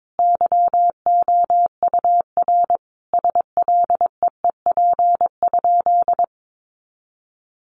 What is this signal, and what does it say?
Morse code, heard as a keyed tone, about 22 wpm, 700 Hz. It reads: YOUR SLEEP?
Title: Sentences from Top 500 Words 22wpm